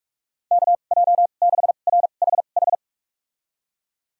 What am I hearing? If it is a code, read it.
Morse code, heard as a keyed tone, about 41 wpm, spaced as for 31 wpm, 700 Hz. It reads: KJ6RHH
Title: USA Call Signs - 6 Characters - 2x Character Spacing - 45wpm